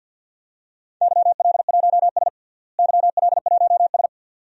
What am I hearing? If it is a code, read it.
Morse code, heard as a keyed tone, about 51 wpm, 700 Hz. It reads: YL1S YL1S
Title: DXCC-100 Call Signs - 4 Characters - Repeated 2x - 50wpm